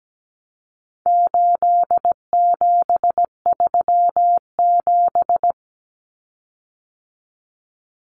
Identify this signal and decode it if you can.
Morse code, heard as a keyed tone, about 17 wpm, 700 Hz. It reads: 8737
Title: Common Cut Numbers - 4-Digit 17wpm